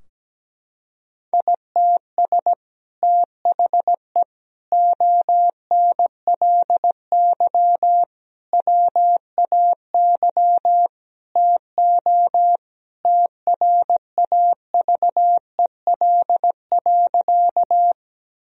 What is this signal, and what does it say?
Morse code, heard as a keyed tone, about 17 wpm, 700 Hz. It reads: ITS THE ONLY WAY TO TRAVEL.
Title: Sentences from Top 1000 Words - Review Part 2 - 17wpm